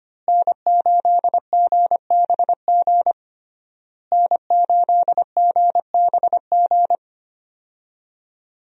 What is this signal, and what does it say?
Morse code, heard as a keyed tone, about 25 wpm, 700 Hz. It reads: N8GBG N8GBG
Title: USA Call Signs - 5 Characters - Repeated 3x - 25wpm